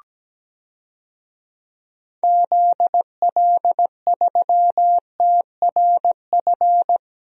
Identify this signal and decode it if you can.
Morse code, heard as a keyed tone, about 17 wpm, 700 Hz. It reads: ZL3TRF
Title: DXCC-100 Call Signs - 6 Characters - 17wpm